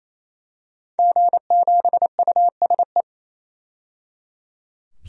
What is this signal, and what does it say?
Morse code, heard as a keyed tone, about 27 wpm, 700 Hz. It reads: G7USE
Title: DXCC-100 Call Signs - 5 Characters - 28wpm